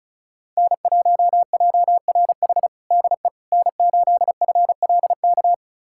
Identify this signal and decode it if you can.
Morse code, heard as a keyed tone, about 34 wpm, 700 Hz. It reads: N1JRH DE N8FLK
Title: USA Callsign DE Callsign - 5 Characters - 35wpm